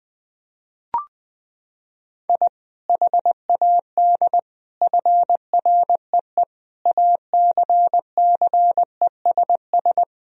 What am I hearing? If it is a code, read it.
Morse code, heard as a keyed tone, about 20 wpm, 700 Hz. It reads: I HAD FREE ACCESS
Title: Sentences from Top 1000 Words - Review Part 4 - 20wpm